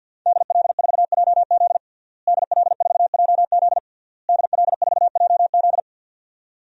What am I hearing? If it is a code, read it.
Morse code, heard as a keyed tone, about 47 wpm, 700 Hz. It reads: DL4JZ DL4JZ DL4JZ
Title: DXCC-100 Call Signs - 5 Characters - Repeated 3x - 50wpm